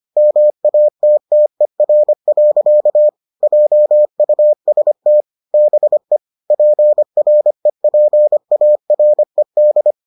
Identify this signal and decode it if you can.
Morse code, heard as a keyed tone, about 25 wpm, 600 Hz. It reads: MATTER. JUST BE PREPARED